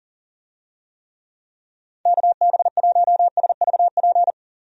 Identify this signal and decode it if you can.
Morse code, heard as a keyed tone, about 38 wpm, 700 Hz. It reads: KB1SVP
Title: USA Call Signs - 6 Characters - 40wpm